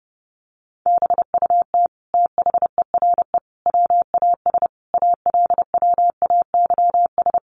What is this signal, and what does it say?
Morse code, heard as a keyed tone, about 30 wpm, 700 Hz. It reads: BUT THERE WAS ALWAYS
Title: Sentences from Top 900 Words 30wpm